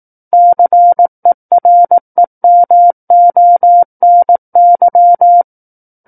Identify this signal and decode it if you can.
Morse code, heard as a keyed tone, about 18 wpm, 700 Hz. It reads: CEREMONY
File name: 18 WPM